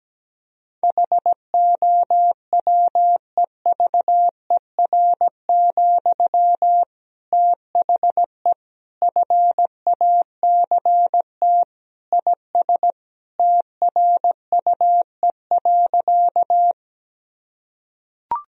Morse code, heard as a keyed tone, about 17 wpm, 700 Hz. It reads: HOWEVER, THE FACT IS TRUE.
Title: Sentences from Top 1000 Words - Review Part 3 - 17wpm